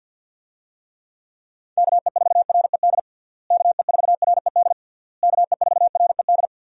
Morse code, heard as a keyed tone, about 52 wpm, 700 Hz. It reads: KE4RED KE4RED KE4RED
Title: USA Call Signs - 6 Characters - Repeated 3x - 50wpm